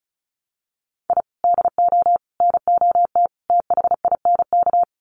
Morse code, heard as a keyed tone, about 34 wpm, 700 Hz. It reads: I DO NOT THINK
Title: Sets of 4 Words 35wpm